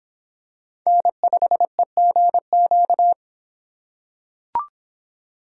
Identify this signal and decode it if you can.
Morse code, heard as a keyed tone, about 24 wpm, 700 Hz. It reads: N5EGQ